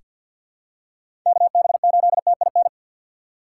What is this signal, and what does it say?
Morse code, heard as a keyed tone, about 48 wpm, 700 Hz. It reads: KB8TIN
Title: USA Call Signs - 4x Character Spacing - 50wpm